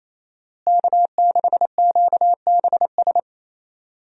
Morse code, heard as a keyed tone, about 28 wpm, 700 Hz. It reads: K6QBS
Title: USA Call Signs - 5 Characters - Repeated 2x - 28wpm